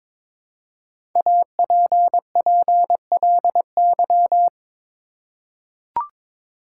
Morse code, heard as a keyed tone, about 22 wpm, 700 Hz. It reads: APPLY
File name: 5 Letter Words - First 3 Letters Given - 22wpm